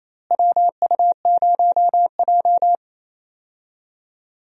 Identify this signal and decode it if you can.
Morse code, heard as a keyed tone, about 28 wpm, 700 Hz. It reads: WU0J
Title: USA Call Signs - 4 Characters - Repeated 3x - 28wpm